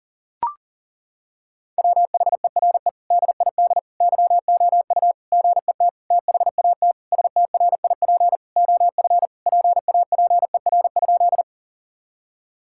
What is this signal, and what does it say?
Morse code, heard as a keyed tone, about 37 wpm, 700 Hz. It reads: WHERE DID YOU GET THAT STRIP OF PAPER?
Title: Sentences from Top 1600 Words 40wpm